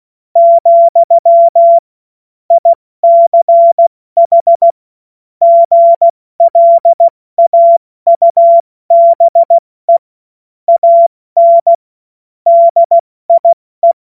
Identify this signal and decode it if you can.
Morse code, heard as a keyed tone, about 16 wpm, spaced as for 14 wpm, 680 Hz. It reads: , ICH GLAUBE AN DIE
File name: Morsekurs / Witze / Witz 09. Februar 2021, 70 F80